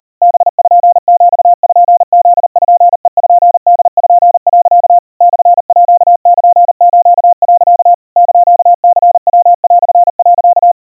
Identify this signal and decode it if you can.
Morse code (Japanese Wabun code), heard as a keyed tone, about 40 wpm, 700 Hz. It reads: パピプペポ、 モールス、 <KN>ニヤ)、
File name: iroha 130 cpm 700 Hz Crispy